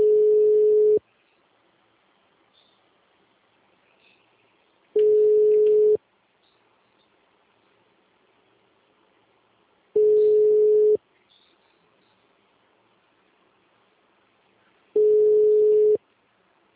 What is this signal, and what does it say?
Ringback tone (Europe, 425 Hz).